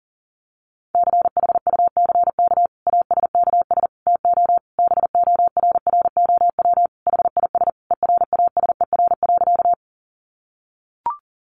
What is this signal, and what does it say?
Morse code, heard as a keyed tone, about 37 wpm, 700 Hz. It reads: CHUCK ASKS TO BORROW HIS ERASER.
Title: Additional Easy Phrases 40wpm